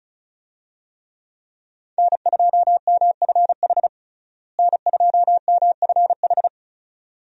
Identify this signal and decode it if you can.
Morse code, heard as a keyed tone, about 36 wpm, 700 Hz. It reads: N2MFH N2MFH
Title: USA Call Signs - 5 Characters - Repeated 2x - 35wpm